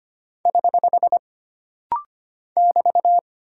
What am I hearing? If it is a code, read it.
Morse code, heard as a keyed tone, about 25 wpm, 700 Hz. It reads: <HH> <BT>
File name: Single Punctuation-Prosign - Rapid-Fire - 25wpm